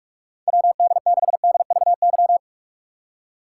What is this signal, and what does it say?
Morse code, heard as a keyed tone, about 47 wpm, 700 Hz. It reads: WD6DVY